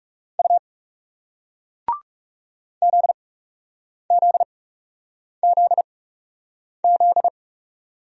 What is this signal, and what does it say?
Morse code, heard as a keyed tone, about 40 wpm, spaced as for 8 wpm, 700 Hz. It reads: U ZZZZ